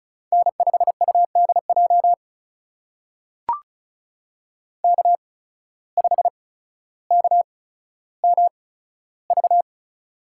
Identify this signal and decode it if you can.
Morse code, heard as a keyed tone, about 36 wpm, spaced as for 15 wpm, 700 Hz. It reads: N5UDJ K5KMV
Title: USA Call Signs - 5 Characters - 8x Character Spacing - 35wpm